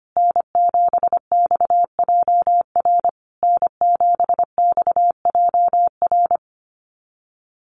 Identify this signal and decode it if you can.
Morse code, heard as a keyed tone, about 25 wpm, 700 Hz. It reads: N7XJR N7XJR
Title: USA Call Signs Repeated 3x 25wpm